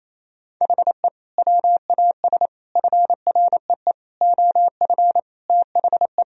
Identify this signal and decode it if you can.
Morse code, heard as a keyed tone, about 28 wpm, 700 Hz. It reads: HE WAS FREE OF THE
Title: Sentences from Top 1000 Words - Review Part 4 - 28wpm